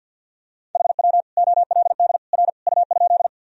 Morse code, heard as a keyed tone, about 50 wpm, 700 Hz. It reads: HW OLD R U?
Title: Sentences from Top 200 Words and Abbreviations 50wpm